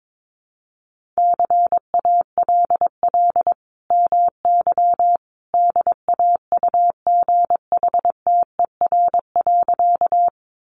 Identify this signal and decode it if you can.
Morse code, heard as a keyed tone, about 22 wpm, 700 Hz. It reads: CALL MY DAUGHTER.
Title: Sentences from Top 1500 Words 22wpm